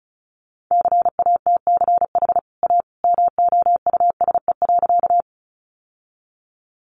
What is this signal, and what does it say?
Morse code, heard as a keyed tone, about 34 wpm, 700 Hz. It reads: CATCH A MOUSE.
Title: Sentences from Top 1500 Words 35wpm